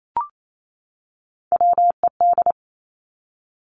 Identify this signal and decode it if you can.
Morse code, heard as a keyed tone, about 28 wpm, 700 Hz. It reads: WED
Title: Three Challenging Letter-Number Combinations 28wpm